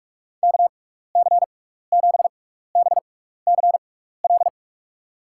Morse code, heard as a keyed tone, about 46 wpm, spaced as for 16 wpm, 700 Hz. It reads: KC7BCL
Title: USA Call Signs - 6x Character Spacing - 45wpm